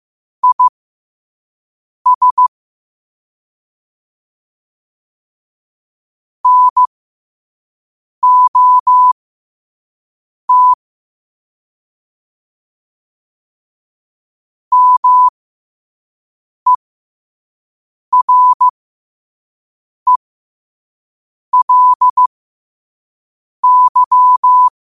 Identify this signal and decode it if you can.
Morse code, heard as a keyed tone, about 15 wpm, spaced as for 5 wpm, 1000 Hz. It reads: IS NOT MERELY
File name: qotdfiles / 5wpm